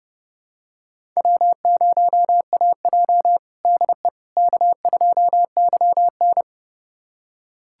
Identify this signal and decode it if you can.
Morse code, heard as a keyed tone, about 30 wpm, 700 Hz. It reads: W0AJ DE K2YN
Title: USA Callsign DE Callsign - 4 Characters - 30wpm